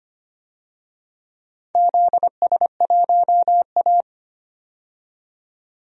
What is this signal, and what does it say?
Morse code, heard as a keyed tone, about 25 wpm, 700 Hz. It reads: ZS1A
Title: DXCC-100 Call Signs - 4 Characters - 25wpm